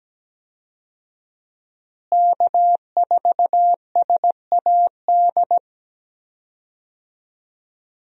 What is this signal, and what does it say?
Morse code, heard as a keyed tone, about 17 wpm, 700 Hz. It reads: K4SAD
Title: USA Call Signs - 5 Characters - Repeated 3x - 17wpm